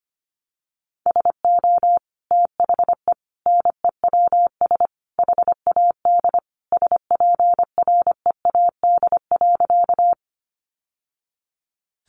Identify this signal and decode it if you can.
Morse code, heard as a keyed tone, about 25 wpm, 700 Hz. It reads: SO THE NEWS HAD SPREAD.
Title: Sentences from Top 1200 Words 25wpm